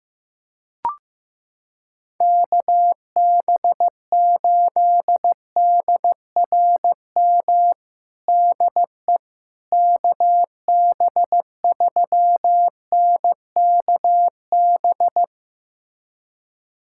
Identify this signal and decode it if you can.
Morse code, heard as a keyed tone, about 16 wpm, 700 Hz. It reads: KB8DRM DE KB3NKB
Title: USA Callsign DE Callsign - 6 Characters - 15wpm